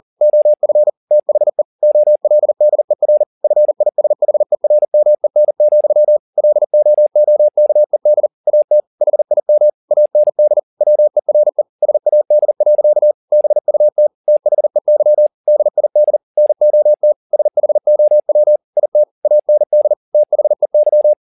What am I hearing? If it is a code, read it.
Morse code, heard as a keyed tone, about 40 wpm, 600 Hz. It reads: OF THE OLDER FISHERMEN, LOOKED AT HIM AND WERE SAD. BUT THEY DID NOT SHOW IT AND THEY
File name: the old man and the sea 1 40wpm